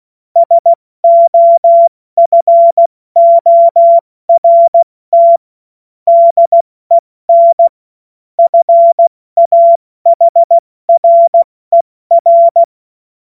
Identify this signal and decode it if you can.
Morse code, heard as a keyed tone, about 16 wpm, spaced as for 14 wpm, 680 Hz. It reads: SOFORT DEN FAHRER
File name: Morsekurs / Witze / Witz 03 Juni Mai 2025 70 F80